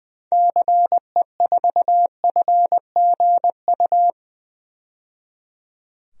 Morse code, heard as a keyed tone, about 20 wpm, 700 Hz. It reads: CE4FGU
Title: DXCC-100 Call Signs - 6 Characters - 20wpm